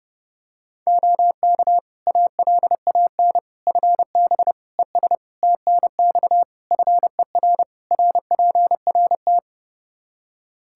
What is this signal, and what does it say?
Morse code, heard as a keyed tone, about 32 wpm, 700 Hz. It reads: OK ALAN FB ES TNX FER RPRT
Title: Practice QSO - Exchange 2 - Info - 30wpm